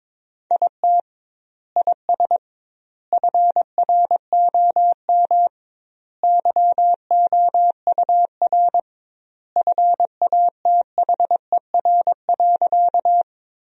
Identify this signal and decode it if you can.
Morse code, heard as a keyed tone, about 22 wpm, 700 Hz. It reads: IT IS FROM YOUR FATHER.
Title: 2x Word-Spacing - Sentences from Top 300 Words - 22wpm